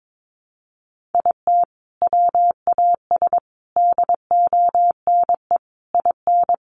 Morse code, heard as a keyed tone, about 22 wpm, 700 Hz. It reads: IT WAS DONE IN
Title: Sentences from Top 1400 Words 22wpm